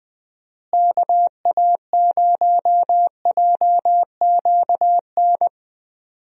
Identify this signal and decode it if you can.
Morse code, heard as a keyed tone, about 20 wpm, 700 Hz. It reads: KA0JQN
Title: USA Call Signs - 6 Characters - Repeated 3x - 20wpm